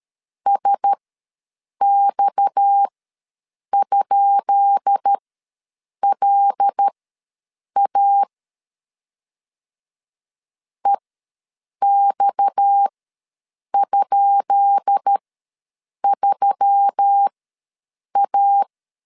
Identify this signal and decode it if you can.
Morse code, heard as a keyed tone, about 13 wpm, spaced as for 7 wpm, 800 Hz. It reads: SX?LA EX?3A
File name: LEZIONI / 10 SIGNS / 10B INT ws 12wpm 3527